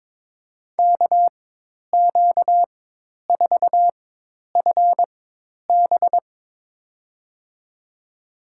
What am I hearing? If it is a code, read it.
Morse code, heard as a keyed tone, about 22 wpm, spaced as for 10 wpm, 700 Hz. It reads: KQ4FB
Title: USA Call Signs - 5 Characters - 4x Character Spacing - 22wpm